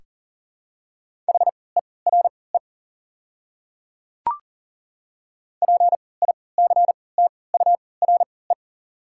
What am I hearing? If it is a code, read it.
Morse code, heard as a keyed tone, about 40 wpm, spaced as for 23 wpm, 700 Hz. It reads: HERE PICTURE
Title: 3x Character Spacing - Top 200 Words - 40wpm